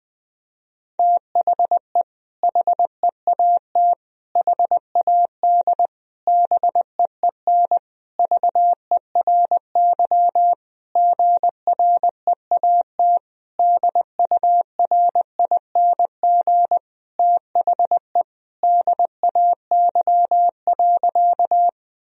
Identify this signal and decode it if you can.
Morse code, heard as a keyed tone, about 19 wpm, 700 Hz. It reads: THE HEAT HAD BEEN VERY GREAT DURING THE DAY.
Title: Sentences from Top 1000 Words - Review Part 3 - 20wpm